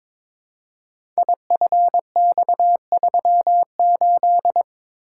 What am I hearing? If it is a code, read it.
Morse code, heard as a keyed tone, about 23 wpm, 700 Hz. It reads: IFX38
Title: Code Groups - 5 Letter:Numbers - 22wpm